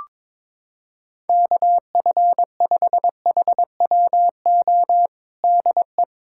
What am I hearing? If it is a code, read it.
Morse code, heard as a keyed tone, about 22 wpm, 700 Hz. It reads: KF5HWO DE
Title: USA Callsign DE Callsign - 6 Characters - 22wpm